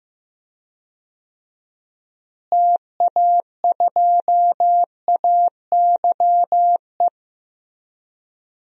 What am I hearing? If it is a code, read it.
Morse code, heard as a keyed tone, about 15 wpm, 700 Hz. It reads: TA2AYE
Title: DXCC-100 Call Signs - 6 Characters - Repeated 2x - 15wpm